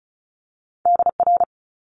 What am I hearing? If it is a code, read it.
Morse code, heard as a keyed tone, about 36 wpm, 700 Hz. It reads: DR
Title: Multi-letter Phonemes Quick Fire 35wpm